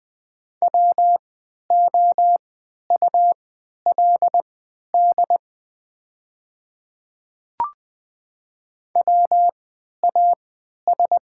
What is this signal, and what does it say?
Morse code, heard as a keyed tone, about 20 wpm, spaced as for 11 wpm, 700 Hz. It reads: WOULD WAS
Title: 3x Character Spacing - Top 100 Words - 20wpm